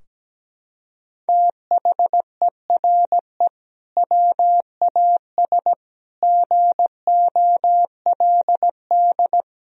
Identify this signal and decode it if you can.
Morse code, heard as a keyed tone, about 17 wpm, 700 Hz. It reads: THERE WAS GOLD
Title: Sentences from Top 1000 Words - Review Part 4 - 17wpm